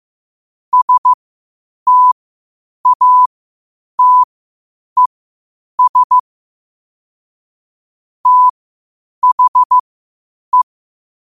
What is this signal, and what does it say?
Morse code, heard as a keyed tone, about 15 wpm, spaced as for 8 wpm, 1000 Hz. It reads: STATES THE
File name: qotdfiles / 8wpm